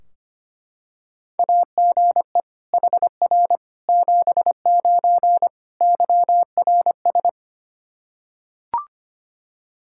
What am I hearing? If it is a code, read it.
Morse code, heard as a keyed tone, about 25 wpm, 700 Hz. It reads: AGE HR 79 YRS
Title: Practice QSO - Exchange 3 - More Info - 25wpm